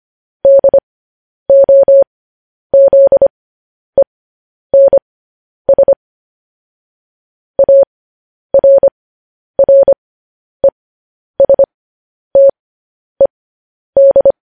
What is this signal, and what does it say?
Morse code, heard as a keyed tone, about 25 wpm, spaced as for 10 wpm, 550 Hz. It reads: DOZENS ARRESTED
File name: podcast / News headlines-1x-25x10